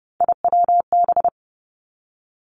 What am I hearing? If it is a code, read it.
Morse code, heard as a keyed tone, about 30 wpm, 700 Hz. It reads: IWB
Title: Three Challenging Letter-Number Combinations 30wpm